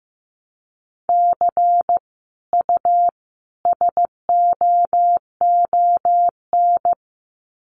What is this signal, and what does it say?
Morse code, heard as a keyed tone, about 15 wpm, 700 Hz. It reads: C U SOON
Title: Common QSO Phrases 15wpm